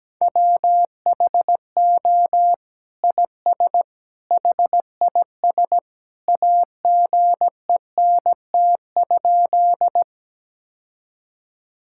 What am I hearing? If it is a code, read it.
Morse code, heard as a keyed tone, about 17 wpm, 700 Hz. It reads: WHO IS HIS AGENT?
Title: Sentences from Top 1600 Words 17wpm